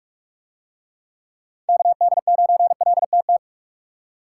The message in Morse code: KD9LTT

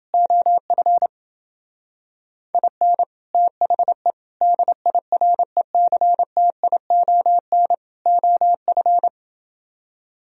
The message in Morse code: OF IN THE DIRECTION OF

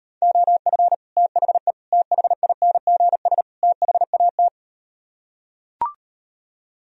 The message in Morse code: OF THE THINGS THAT

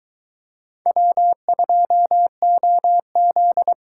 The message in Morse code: W2OZ

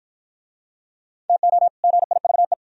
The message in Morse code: TO GIVE

